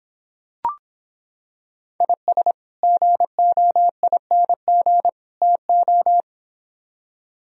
Morse code, heard as a keyed tone, about 26 wpm, 700 Hz. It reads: IS GOING TO